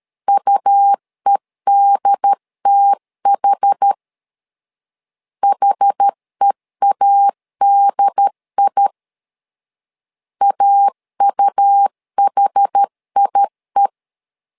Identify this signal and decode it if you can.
Morse code, heard as a keyed tone, about 13 wpm, 800 Hz. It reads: UEDTH HEADI AUHIE
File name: LEZIONI / 2 UHDN / 2 UHD 12wpm 1540